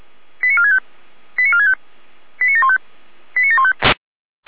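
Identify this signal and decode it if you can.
Five-tone selective call (ZVEI), heard as quick runs of five tones, tones of about 80 ms. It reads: 89465 89465 89715 89715
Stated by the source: ZVEI five-tone call received by SDR.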